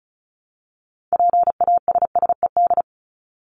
Morse code, heard as a keyed tone, about 34 wpm, 700 Hz. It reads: PASSED
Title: ED Suffix - Top 100 Words with Base Words 35wpm